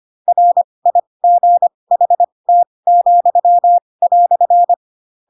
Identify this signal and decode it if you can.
Morse code, heard as a keyed tone, about 25 wpm, spaced as for 20 wpm, 700 Hz. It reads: RIGHT,"
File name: Pinocchio / Pinocchio ch7 15 25 ik1hge